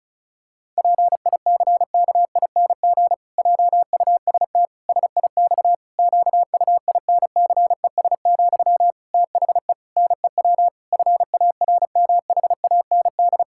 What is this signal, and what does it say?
Morse code, heard as a keyed tone, about 36 wpm, 700 Hz. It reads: PICKING JUST SIX QUINCES, THE NEW FARMHAND
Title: Pangram Challenge v2 - 35wpm